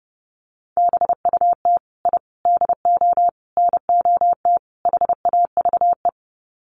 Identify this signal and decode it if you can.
Morse code, heard as a keyed tone, about 30 wpm, 700 Hz. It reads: BUT I DO NOT HAVE
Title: Sets of 5 Words Encore 30wpm